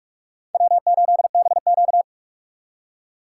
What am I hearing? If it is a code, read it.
Morse code, heard as a keyed tone, about 42 wpm, 700 Hz. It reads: W8BQ
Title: USA Call Signs - 4x Character Spacing - 45wpm